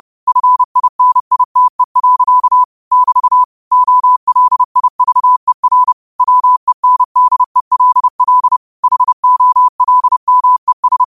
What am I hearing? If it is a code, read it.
Morse code, heard as a keyed tone, about 30 wpm, 1000 Hz. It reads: FINITE. <BT> OLIVER WENDELL HOLMES
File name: qotdfiles / 30wpm